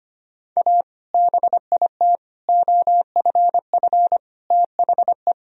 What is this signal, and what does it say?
Morse code, heard as a keyed tone, about 25 wpm, 700 Hz. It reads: A BIT OFF THE
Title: Sentences from Top 1000 Words - Review Part 2 - 25wpm